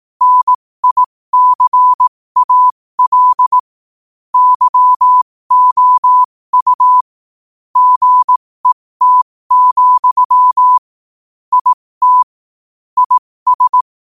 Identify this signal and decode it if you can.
Morse code, heard as a keyed tone, about 18 wpm, spaced as for 15 wpm, 1000 Hz. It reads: NICAL YOU GET, IT IS
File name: qotdfiles / 15wpm